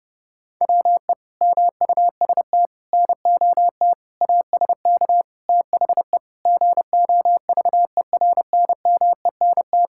Morse code, heard as a keyed tone, about 30 wpm, 700 Hz. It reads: WE MUST NOT ASK THE GOVERNMENT
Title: Sentences from Top 1600 Words 30wpm